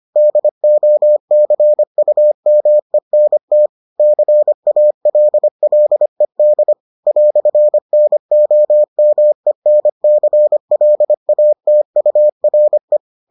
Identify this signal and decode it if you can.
Morse code, heard as a keyed tone, about 25 wpm, 600 Hz. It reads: DOCUMENT CALLED "NOMENCLATURE